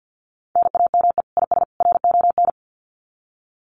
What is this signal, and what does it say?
Morse code, heard as a keyed tone, about 49 wpm, 700 Hz. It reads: NAME IS RON
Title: Common QSO Phrases 50wpm